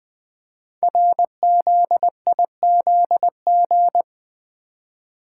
Morse code, heard as a keyed tone, about 20 wpm, 700 Hz. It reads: RZIZG
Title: Code Groups - 5 Letters - 20wpm